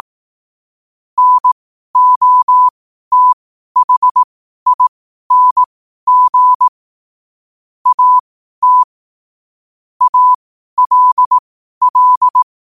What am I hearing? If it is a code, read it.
Morse code, heard as a keyed tone, about 18 wpm, spaced as for 12 wpm, 1000 Hz. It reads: NOTHING AT ALL
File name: qotdfiles / 12wpm